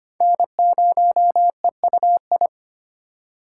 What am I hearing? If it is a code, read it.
Morse code, heard as a keyed tone, about 25 wpm, 700 Hz. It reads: N0EUI
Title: USA Call Signs - 5 Characters - 4x Character Spacing - 25wpm